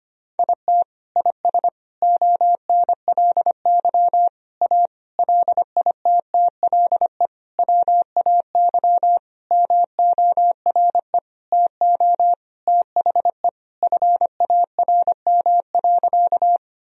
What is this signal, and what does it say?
Morse code, heard as a keyed tone, about 25 wpm, 700 Hz. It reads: IT IS ONLY A LITTLE WAY MORE TO THE FARM.
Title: Sentences from Top 1000 Words - Review Part 2 - 25wpm